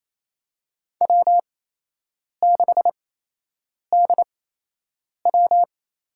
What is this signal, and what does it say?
Morse code, heard as a keyed tone, about 28 wpm, spaced as for 8 wpm, 700 Hz. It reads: W6DW
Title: USA Call Signs - 8x Character Spacing - 28wpm